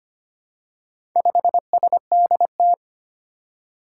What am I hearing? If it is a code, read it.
Morse code, heard as a keyed tone, about 25 wpm, 700 Hz. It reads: 5SDT
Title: Four Challenging Letter-Number Combinations 25wpm